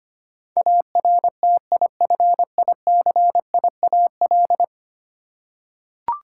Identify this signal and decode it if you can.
Morse code, heard as a keyed tone, about 25 wpm, 700 Hz. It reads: ARTIFICIAL
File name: 10 Letter Words - First 3 Letters Given - 25wpm